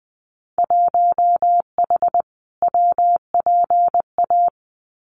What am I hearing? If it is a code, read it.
Morse code, heard as a keyed tone, about 20 wpm, 700 Hz. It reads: 1H WPA
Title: Winter Field Day Exchanges 20wpm